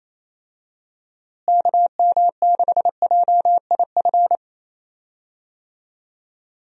Morse code, heard as a keyed tone, about 28 wpm, 700 Hz. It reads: KM6JIF
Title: USA Call Signs - 6 Characters - Repeated 2x - 28wpm